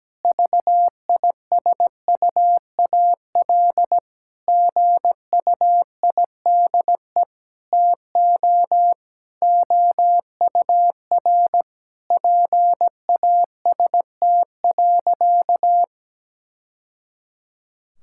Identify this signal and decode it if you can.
Morse code, heard as a keyed tone, about 17 wpm, 700 Hz. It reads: VISUAL GUIDE TO OUR PAST.
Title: Sentences from Top 1000 Words - Review Part 6 - 17wpm